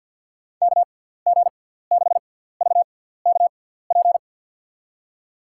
Morse code, heard as a keyed tone, about 49 wpm, spaced as for 17 wpm, 700 Hz. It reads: KG6VKP
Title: USA Call Signs - 6 Characters - 6x Character Spacing - 50wpm